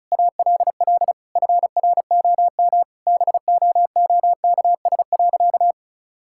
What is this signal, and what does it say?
Morse code, heard as a keyed tone, about 34 wpm, 700 Hz. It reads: ALL FROM BOOKS.
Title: Sentences from Top 1000 Words - Review Part 6 - 35wpm